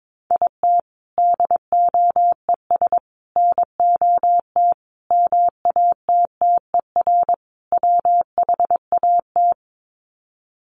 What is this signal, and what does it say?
Morse code, heard as a keyed tone, about 22 wpm, 700 Hz. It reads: IT DOES NOT MATTER WHAT
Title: Sets of 5 Words Encore 22wpm